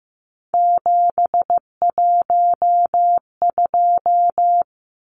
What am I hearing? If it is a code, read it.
Morse code, heard as a keyed tone, about 15 wpm, 700 Hz. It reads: 712
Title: Numbers - 3-Digit 15wpm